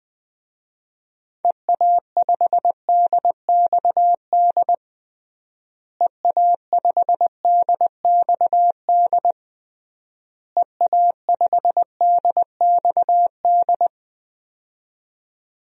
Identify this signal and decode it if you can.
Morse code, heard as a keyed tone, about 20 wpm, 700 Hz. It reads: EA5DXD EA5DXD EA5DXD
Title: DXCC-100 Call Signs - 6 Characters - Repeated 3x - 20wpm